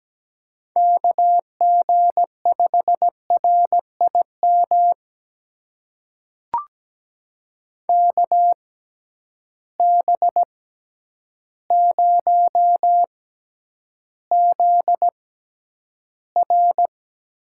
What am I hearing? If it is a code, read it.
Morse code, heard as a keyed tone, about 17 wpm, spaced as for 9 wpm, 700 Hz. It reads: KG5RIM KB0ZR
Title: USA Call Signs - 6 Characters - 6x Character Spacing - 17wpm